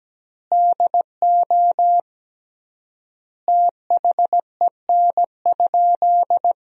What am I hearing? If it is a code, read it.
Morse code, heard as a keyed tone, about 17 wpm, 700 Hz. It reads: DO THEN?
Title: 3x Word-Spacing - Sentences from Top 100 Words - 17wpm